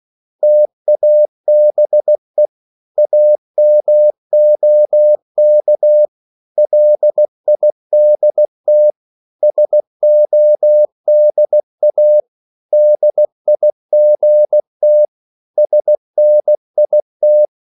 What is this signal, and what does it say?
Morse code, heard as a keyed tone, about 16 wpm, 600 Hz. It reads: TABE AMOK LIDT SODA DIGT SNIT
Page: Korte danske ord 16wpm